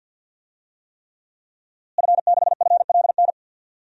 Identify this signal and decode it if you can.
Morse code, heard as a keyed tone, about 52 wpm, 700 Hz. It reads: F6FLN
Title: DXCC-100 Call Signs - 5 Characters - Repeated 3x - 50wpm